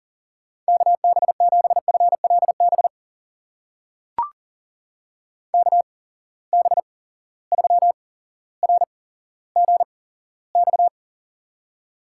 Morse code, heard as a keyed tone, about 39 wpm, spaced as for 17 wpm, 700 Hz. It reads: KB7FLB KB3RGX
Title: USA Call Signs - 6 Characters - 8x Character Spacing - 40wpm